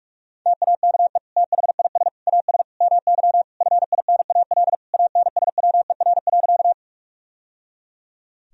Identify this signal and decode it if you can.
Morse code, heard as a keyed tone, about 42 wpm, 700 Hz. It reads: TAKE THIS AS MY FINAL ANSWER.
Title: Sentences from Top 1000 Words - Review Part 2 - 45wpm